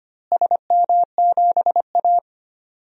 Morse code, heard as a keyed tone, about 25 wpm, 700 Hz. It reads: SM7A
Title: DXCC-100 Call Signs - 4 Characters - Repeated 3x - 25wpm